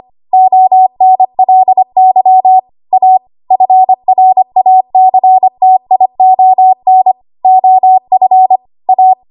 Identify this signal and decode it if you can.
Morse code, heard as a keyed tone, about 25 wpm, 750 Hz. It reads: ONLY A FRACTION OF A
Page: W1AW Web Code Practice Files